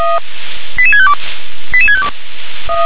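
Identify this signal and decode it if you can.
Five-tone selective call (ZVEI), heard as quick runs of five tones, tones of about 70 ms. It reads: 88652 88652
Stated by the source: ZVEI five-tone selective call received on SDR.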